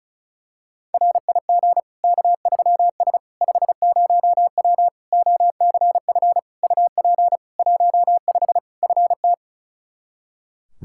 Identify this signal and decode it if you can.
Morse code, heard as a keyed tone, about 36 wpm, 700 Hz. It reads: RIG K3S 50W OCF UP 15 FT
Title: K1USN SST QSO Practice 35wpm